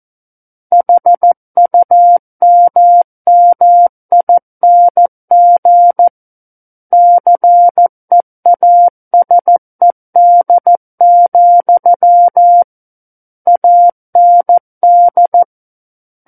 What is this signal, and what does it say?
Morse code, heard as a keyed tone, about 14 wpm, 700 Hz. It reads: HUMMING CEASED, AND
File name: Book One / 05-Chapter 5 - 14 WPM